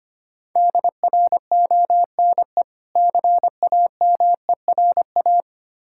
Morse code, heard as a keyed tone, about 25 wpm, 700 Hz. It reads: DRONE CAMERA
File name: practice / 25wpm / Difficult Words with Hints - 25wpm